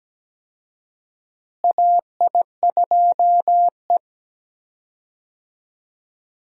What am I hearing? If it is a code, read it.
Morse code, heard as a keyed tone, about 17 wpm, 700 Hz. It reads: AI2E
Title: USA Call Signs - 4 Characters - Repeated 3x - 17wpm